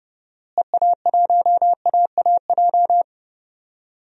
Morse code, heard as a keyed tone, about 31 wpm, 700 Hz. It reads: EA1AAJ
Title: DXCC-100 Call Signs - 6 Characters - Repeated 2x - 30wpm